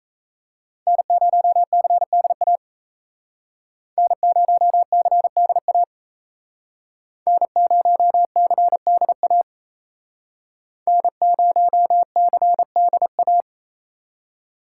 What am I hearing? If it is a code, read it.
Morse code, heard as a keyed tone, about 35 wpm, 700 Hz. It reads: N0CDA N0CDA N0CDA N0CDA